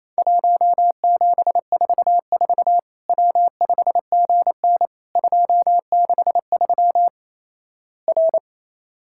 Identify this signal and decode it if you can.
Morse code, heard as a keyed tone, about 28 wpm, 700 Hz. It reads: 1744 W5GN 263 R
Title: Worked All Europe DX Contest - QTC Exchange - 28wpm